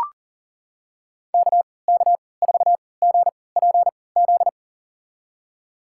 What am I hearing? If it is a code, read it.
Morse code, heard as a keyed tone, about 37 wpm, spaced as for 22 wpm, 700 Hz. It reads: KK4GPZ
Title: USA Call Signs - 6 Characters - 3x Character Spacing - 40wpm